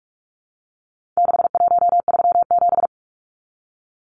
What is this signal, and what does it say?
Morse code, heard as a keyed tone, about 44 wpm, 700 Hz. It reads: 6137
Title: Numbers - 4-Digit 45wpm